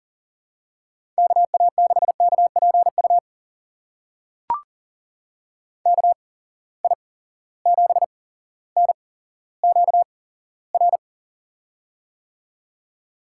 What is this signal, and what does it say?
Morse code, heard as a keyed tone, about 39 wpm, spaced as for 17 wpm, 700 Hz. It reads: KA6KPU KI7NQR
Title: USA Call Signs - 6 Characters - 8x Character Spacing - 40wpm